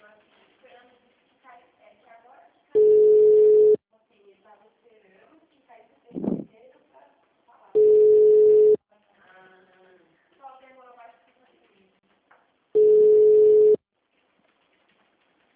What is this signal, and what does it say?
Ringback tone (Europe, 425 Hz), with speech.
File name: out- -2012- - - .310